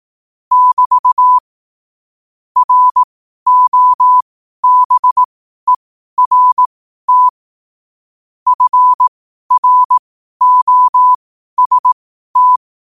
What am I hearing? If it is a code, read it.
Morse code, heard as a keyed tone, about 18 wpm, spaced as for 12 wpm, 1000 Hz. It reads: <BT> ROBERT FROST
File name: qotdfiles / 12wpm